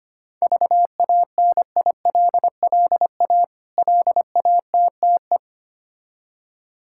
Morse code, heard as a keyed tone, about 25 wpm, 700 Hz. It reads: VANILLA LATTE
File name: practice / 25wpm / Difficult Words with Hints - 25wpm